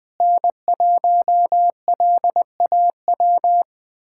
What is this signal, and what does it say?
Morse code, heard as a keyed tone, about 20 wpm, 700 Hz. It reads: N1LAW
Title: USA Callsign DE Callsign - 5 Characters - 20wpm